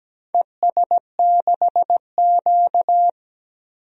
Morse code, heard as a keyed tone, about 17 wpm, 700 Hz. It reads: ES6Q
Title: DXCC-100 Call Signs - 4 Characters - 17wpm